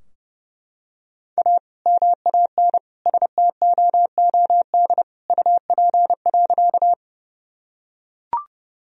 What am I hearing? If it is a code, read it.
Morse code, heard as a keyed tone, about 32 wpm, 700 Hz. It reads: A MAN STOOD UP.
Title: Sentences from Top 1000 Words - Review Part 2 - 30wpm